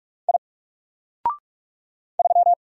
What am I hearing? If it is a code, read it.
Morse code, heard as a keyed tone, about 44 wpm, 700 Hz. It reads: I 3
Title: Single Letter-Number Alt 45wpm